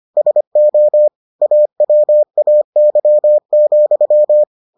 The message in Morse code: SO AWAY,

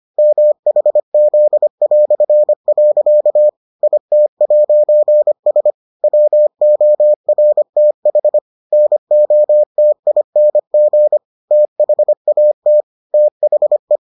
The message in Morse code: MHZ". IT'S WORTH NOTING THAT THE

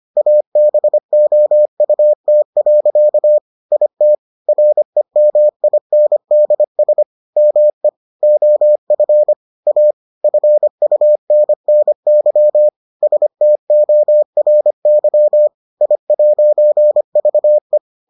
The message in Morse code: ABOUT. IT REMINDS ME OF A FUNNY STORY I'VE

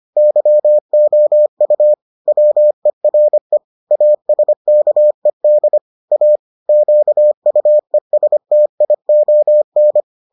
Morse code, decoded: YOU WERE ASKED A QUESTION

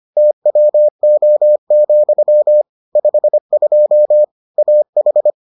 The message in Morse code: TWO, 52 AH

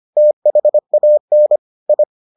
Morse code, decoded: THAN I

